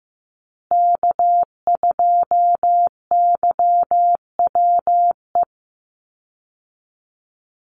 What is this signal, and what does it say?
Morse code, heard as a keyed tone, about 14 wpm, 700 Hz. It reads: K2YWE
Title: Call Signs Repeated 2x - 15wpm